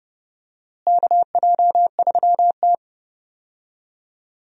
Morse code, heard as a keyed tone, about 31 wpm, 700 Hz. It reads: KJ3T
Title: USA Call Signs - 4 Characters - Repeated 3x - 30wpm